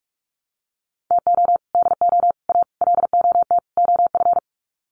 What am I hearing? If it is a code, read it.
Morse code, heard as a keyed tone, about 44 wpm, 700 Hz. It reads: TO DO A LOT OF
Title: Sets of 5 Words Encore 45wpm